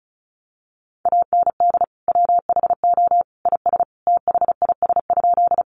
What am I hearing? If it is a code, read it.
Morse code, heard as a keyed tone, about 34 wpm, 700 Hz. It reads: AND WHO IS THIS?
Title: Sentences from Top 100 Words 35wpm